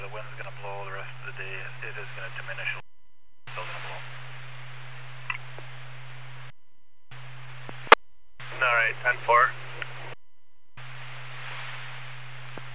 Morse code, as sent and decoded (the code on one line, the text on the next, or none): none